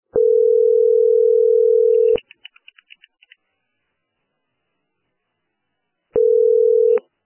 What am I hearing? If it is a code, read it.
Ringback tone (North America), 440 + 480 Hz.